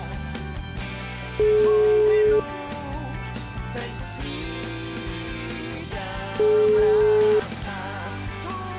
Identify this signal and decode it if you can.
Ringback tone (Europe, 425 Hz).